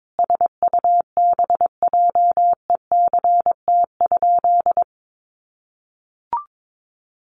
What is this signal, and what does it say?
Morse code, heard as a keyed tone, about 22 wpm, 700 Hz. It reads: SUBJECT?
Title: Sentences from Top 1200 Words 22wpm